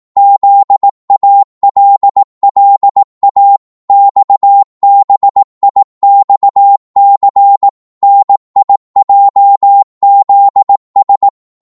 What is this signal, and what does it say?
Morse code, heard as a keyed tone, about 18 wpm, 800 Hz. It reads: ZALLA XBIXC NIJZS
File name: Morse / ltr-18wpm